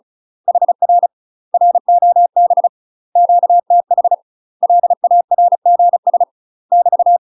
Morse code, heard as a keyed tone, about 34 wpm, 700 Hz. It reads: HR ROB QTH LARGS <BT>
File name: UK MASTER QSOs TRAINING / 35WPM / UK MASTER QSOs 0034